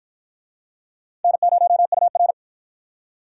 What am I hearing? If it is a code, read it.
Morse code, heard as a keyed tone, about 53 wpm, 700 Hz. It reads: N0UR